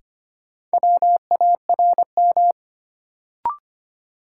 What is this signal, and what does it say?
Morse code, heard as a keyed tone, about 25 wpm, 700 Hz. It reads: WARM
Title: Sentences from Top 500 Words - New Word Drill - 25wpm